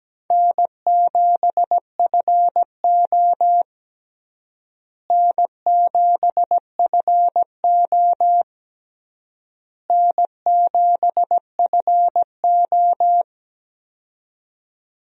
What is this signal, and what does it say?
Morse code, heard as a keyed tone, about 17 wpm, 700 Hz. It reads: N7FO N7FO N7FO
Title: USA Call Signs - 4 Characters - Repeated 3x - 17wpm